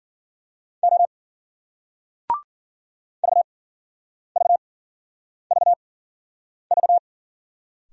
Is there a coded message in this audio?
Yes: Morse code from 0.8 to 7.0 s, heard as a keyed tone, about 52 wpm, spaced as for 9 wpm, 700 Hz. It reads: O VVVV